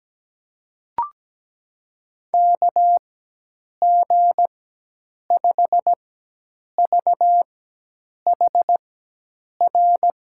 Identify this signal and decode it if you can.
Morse code, heard as a keyed tone, about 17 wpm, spaced as for 8 wpm, 700 Hz. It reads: KG5VHR
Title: USA Call Signs - 4x Character Spacing - 17wpm